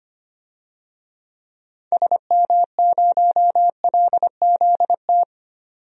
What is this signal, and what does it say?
Morse code, heard as a keyed tone, about 25 wpm, 700 Hz. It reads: SM0LZT
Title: DXCC-100 Call Signs - 6 Characters - 25wpm